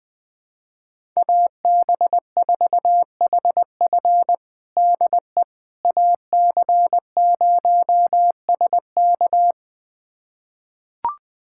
Morse code, heard as a keyed tone, about 21 wpm, 700 Hz. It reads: AB4HF DE AC0SK
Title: USA Callsign DE Callsign - 5 Characters - 20wpm